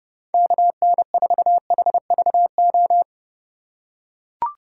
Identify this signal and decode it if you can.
Morse code, heard as a keyed tone, about 32 wpm, 700 Hz. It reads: KN4HVO
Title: USA Call Signs - 6 Characters - 2x Character Spacing - 30wpm